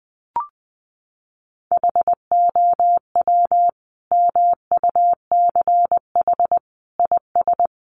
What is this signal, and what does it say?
Morse code, heard as a keyed tone, about 20 wpm, 700 Hz. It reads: HOW MUCH IS
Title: Sentences from Top 1200 Words 20wpm